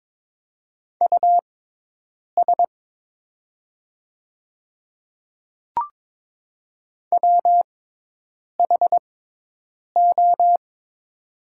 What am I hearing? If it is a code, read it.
Morse code, heard as a keyed tone, about 22 wpm, spaced as for 8 wpm, 700 Hz. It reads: US WHO
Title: 6x Character Spacing - Top 300 Words - 22wpm